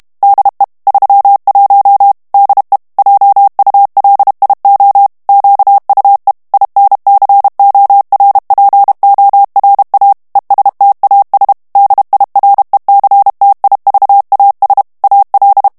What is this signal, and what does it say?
Morse code, heard as a keyed tone, about 32 wpm, 800 Hz. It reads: DE 31 DE JULIO QUE INCORPORA ESTAS DIRECTIVAS AL
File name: ESPAGNOL TEXTE EN CLAIR 2 VITESSE 30 MOTS MINUTES